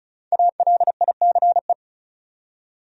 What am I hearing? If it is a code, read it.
Morse code, heard as a keyed tone, about 36 wpm, 700 Hz. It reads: ALICE
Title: USA Common Names - Max 5 Letters - 35wpm